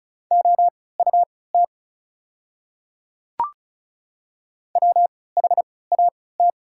Morse code, heard as a keyed tone, about 36 wpm, spaced as for 20 wpm, 700 Hz. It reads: OUT WHAT
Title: 3x Character Spacing - Top 500 Words - 35wpm